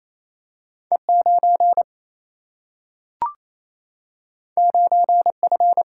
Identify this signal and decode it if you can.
Morse code, heard as a keyed tone, about 28 wpm, 700 Hz. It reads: E 9 9 F
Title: Single-Letter Single-Number 28wpm